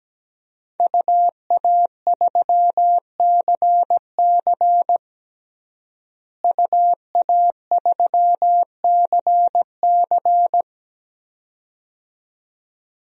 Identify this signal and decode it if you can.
Morse code, heard as a keyed tone, about 17 wpm, 700 Hz. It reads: UA3CC UA3CC
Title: DXCC-100 Call Signs - 5 Characters - Repeated 2x - 17wpm